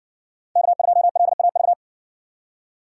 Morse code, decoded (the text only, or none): K2LAV